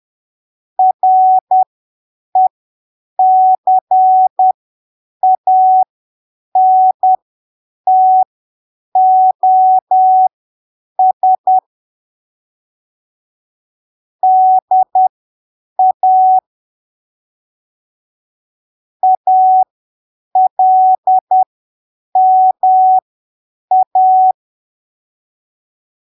Morse code, heard as a keyed tone, about 10 wpm, spaced as for 7 wpm, 750 Hz. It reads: RECANTOS DA ALMA
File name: arquivos / texto / texto 29 10ppm